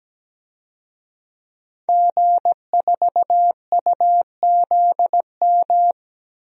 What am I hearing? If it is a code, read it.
Morse code, heard as a keyed tone, about 17 wpm, 700 Hz. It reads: G4UZM
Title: DXCC-100 Call Signs - 5 Characters - 17wpm